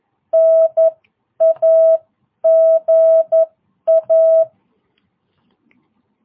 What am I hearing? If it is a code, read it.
Morse code, heard as a keyed tone, about 11 wpm, spaced as for 9 wpm, 660 Hz. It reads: NAGA
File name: jh8cbh / 2022cw